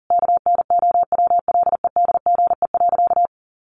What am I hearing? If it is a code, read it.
Morse code, heard as a keyed tone, about 40 wpm, 700 Hz. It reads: KNOWLEDGE.
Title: Sentences from Top 1300 Words 40wpm